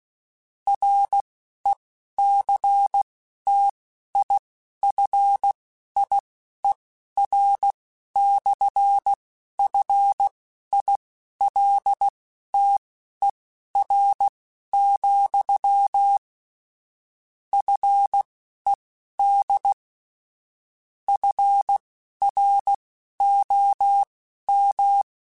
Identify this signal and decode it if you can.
Morse code, heard as a keyed tone, about 16 wpm, spaced as for 11 wpm, 800 Hz. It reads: RECTIFIER/FILTER, FED FROM
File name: Technical Subjects / Technical Subjects-mp3s / Sw Pwr Sup @10